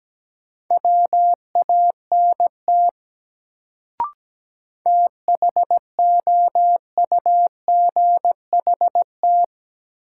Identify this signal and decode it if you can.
Morse code, heard as a keyed tone, about 17 wpm, 700 Hz. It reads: WANT THOUGHT
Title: Sets of 2 Words - New Word Drill - 17wpm